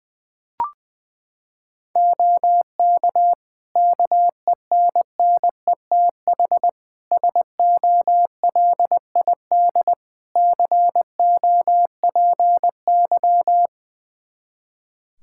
Morse code, heard as a keyed tone, about 20 wpm, 700 Hz. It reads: OK KENNETH SOLID COPY
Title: Practice QSO - Exchange 3 - More Info - 20wpm